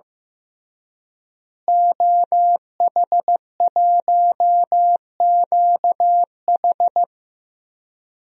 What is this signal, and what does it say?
Morse code, heard as a keyed tone, about 16 wpm, 700 Hz. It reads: OH1QH
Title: DXCC-100 Call Signs - 5 Characters - Repeated 3x - 15wpm